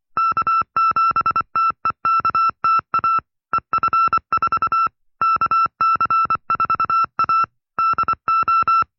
Morse code, heard as a keyed tone, about 24 wpm, 1340 Hz. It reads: K7TEKTA EF4 KC4A DO
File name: FreeFiles / CW-practice / 20WPM-side2